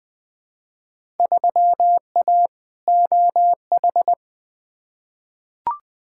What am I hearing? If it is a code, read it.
Morse code, heard as a keyed tone, about 20 wpm, 700 Hz. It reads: 3A OH
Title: Field Day Exchanges - 20wpm